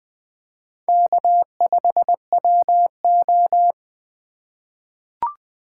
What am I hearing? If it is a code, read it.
Morse code, heard as a keyed tone, about 20 wpm, 700 Hz. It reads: K5WO
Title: USA Call Signs - 4 Characters - 4x Character Spacing - 20wpm